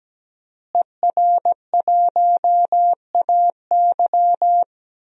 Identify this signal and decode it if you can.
Morse code, heard as a keyed tone, about 17 wpm, 700 Hz. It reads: ER1AY
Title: DXCC-100 Call Signs - 5 Characters - Repeated 2x - 17wpm